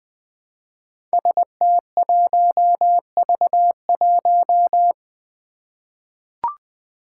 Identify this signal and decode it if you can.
Morse code, heard as a keyed tone, about 20 wpm, 700 Hz. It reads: ST1V1
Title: Five Challenging Letter-Number Combinations 20wpm